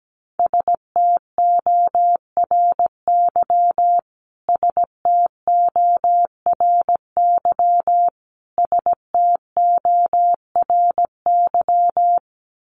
Morse code, heard as a keyed tone, about 17 wpm, 700 Hz. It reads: STORY STORY STORY
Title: Top 500 Words Repeated 17wpm